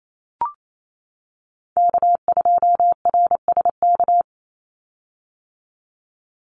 Morse code, heard as a keyed tone, about 28 wpm, 700 Hz. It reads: K2RSK
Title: Call Signs - 28wpm